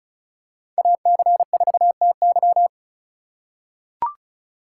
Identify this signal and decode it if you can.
Morse code, heard as a keyed tone, about 36 wpm, 690 Hz. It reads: AC4TY
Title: USA Call Signs - 5 Characters - 3x Character Spacing - 35wpm